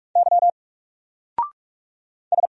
Morse code, heard as a keyed tone, about 44 wpm, 700 Hz. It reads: Y S
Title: Single Letter-Number 45wpm